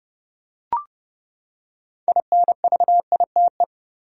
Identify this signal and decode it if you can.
Morse code, heard as a keyed tone, about 29 wpm, 700 Hz. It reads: INVITE
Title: Six Letter Words - 30wpm